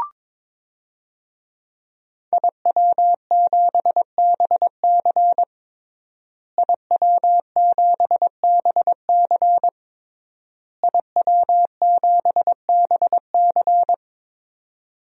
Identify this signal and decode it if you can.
Morse code, heard as a keyed tone, about 22 wpm, 700 Hz. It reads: IW7BC IW7BC IW7BC
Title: DXCC-100 Call Signs - 5 Characters - Repeated 3x - 22wpm